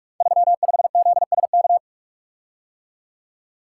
Morse code, heard as a keyed tone, about 44 wpm, 700 Hz. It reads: 35ZSK
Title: Code Groups - 5 Letter:Numbers - 45wpm